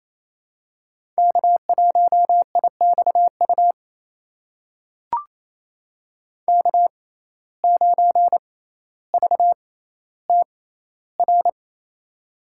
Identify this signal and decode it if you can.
Morse code, heard as a keyed tone, about 28 wpm, spaced as for 14 wpm, 700 Hz. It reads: K1IXU K9VTR
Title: USA Call Signs - 5 Characters - 6x Character Spacing - 28wpm